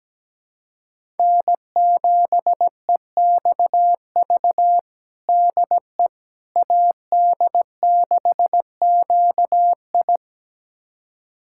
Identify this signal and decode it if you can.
Morse code, heard as a keyed tone, about 17 wpm, 700 Hz. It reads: N7EXV DE AD6QI
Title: USA Callsign DE Callsign - 5 Characters - 17wpm